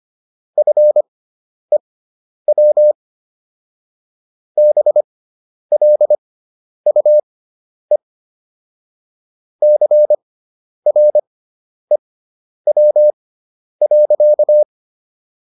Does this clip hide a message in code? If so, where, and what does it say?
Yes: Morse code from 0.6 to 14.6 s, heard as a keyed tone, about 25 wpm, spaced as for 10 wpm, 600 Hz. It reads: FEW BLUE CREW.